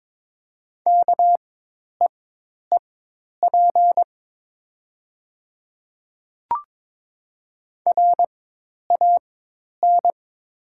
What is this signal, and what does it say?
Morse code, heard as a keyed tone, about 22 wpm, spaced as for 10 wpm, 700 Hz. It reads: KEEP RAN
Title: 4x Character Spacing - Top 500 Words - 22wpm